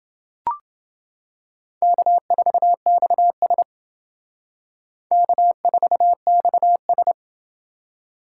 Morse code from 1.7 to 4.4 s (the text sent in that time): K4XS